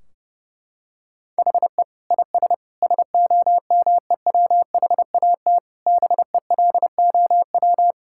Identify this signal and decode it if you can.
Morse code, heard as a keyed tone, about 30 wpm, 700 Hz. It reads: HE IS SOMEWHAT BELOW
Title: Sentences from Top 1600 Words 30wpm